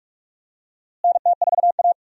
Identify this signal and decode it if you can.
Morse code, heard as a keyed tone, about 46 wpm, 700 Hz. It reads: NT4A
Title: USA Call Signs - 4 Characters - Repeated 2x - 45wpm